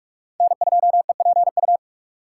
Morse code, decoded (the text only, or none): N2EPU